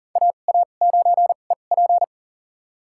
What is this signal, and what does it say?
Morse code, heard as a keyed tone, about 40 wpm, spaced as for 29 wpm, 700 Hz. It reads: AA9EP
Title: USA Call Signs - 5 Characters - 2x Character Spacing - 40wpm